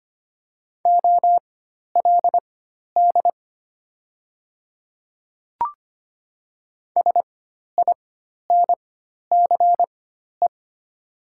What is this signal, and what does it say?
Morse code, heard as a keyed tone, about 25 wpm, spaced as for 12 wpm, 700 Hz. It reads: OLD SINCE
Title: 4x Character Spacing - Top 500 Words - 25wpm